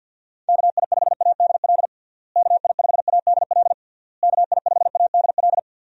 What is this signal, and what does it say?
Morse code, heard as a keyed tone, about 52 wpm, 700 Hz. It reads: KI5ADL KI5ADL KI5ADL
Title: USA Call Signs - 6 Characters - Repeated 3x - 50wpm